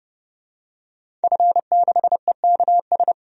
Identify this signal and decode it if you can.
Morse code, heard as a keyed tone, about 30 wpm, 700 Hz. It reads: F6EKS
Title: DXCC-100 Call Signs - 5 Characters - 30wpm